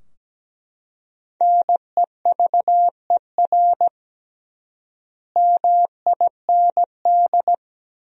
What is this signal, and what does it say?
Morse code, heard as a keyed tone, about 17 wpm, 700 Hz. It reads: NEVER MIND
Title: 3x Word-Spacing - Sentences from Top 500 Words - 17wpm